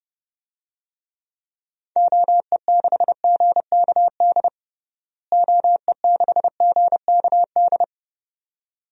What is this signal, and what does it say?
Morse code, heard as a keyed tone, about 30 wpm, 700 Hz. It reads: OE6GKD OE6GKD
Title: DXCC-100 Call Signs - 6 Characters - Repeated 2x - 30wpm